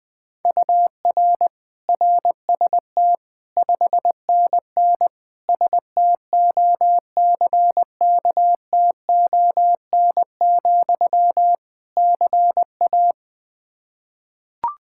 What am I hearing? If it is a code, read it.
Morse code, heard as a keyed tone, about 20 wpm, 700 Hz. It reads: UR RST 5NN STOCKTON, CA
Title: K1USN SST QSO Practice 20wpm